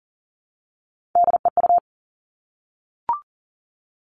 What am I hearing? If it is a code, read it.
Morse code, heard as a keyed tone, about 42 wpm, 700 Hz. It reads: DEU
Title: Three Challenging Letter-Number Combinations 40wpm